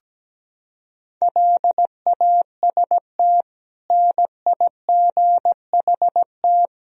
Morse code, heard as a keyed tone, about 17 wpm, 700 Hz. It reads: LAST NIGHT
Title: Sentences from Top 1000 Words - Review Part 4 - 17wpm